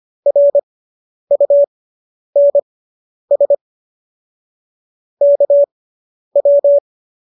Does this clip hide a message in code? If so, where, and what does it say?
Yes: Morse code from 0.3 to 6.8 s, heard as a keyed tone, about 25 wpm, spaced as for 10 wpm, 570 Hz. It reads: RUNS KW